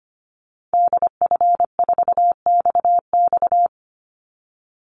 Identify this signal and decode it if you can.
Morse code, heard as a keyed tone, about 25 wpm, 700 Hz. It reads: DF4XX
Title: Call Signs Repeated 3x - 25wpm